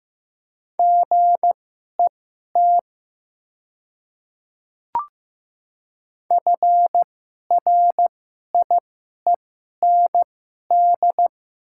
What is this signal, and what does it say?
Morse code, heard as a keyed tone, about 15 wpm, spaced as for 11 wpm, 700 Hz. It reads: GET FRIEND
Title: 2x Character Spacing - Top 300 Words - 15wpm